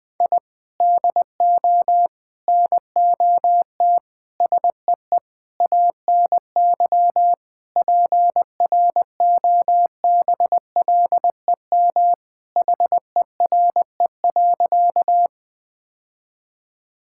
Morse code, heard as a keyed tone, about 21 wpm, 700 Hz. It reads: I DO NOT SEE ANY PROBLEM HERE.
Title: Sentences with 5-Word Phrases - 20wpm